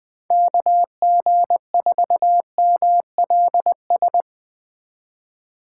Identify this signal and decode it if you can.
Morse code, heard as a keyed tone, about 19 wpm, 700 Hz. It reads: KG4MLS